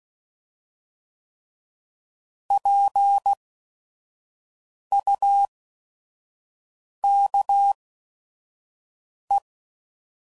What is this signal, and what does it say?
Morse code, heard as a keyed tone, about 16 wpm, spaced as for 5 wpm, 800 Hz. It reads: PUKE
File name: Prefixes / By First Letter / Ps